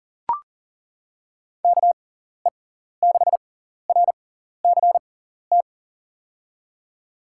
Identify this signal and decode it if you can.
Morse code, heard as a keyed tone, about 40 wpm, spaced as for 14 wpm, 700 Hz. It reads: KE6RCT
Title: USA Call Signs - 6 Characters - 6x Character Spacing - 40wpm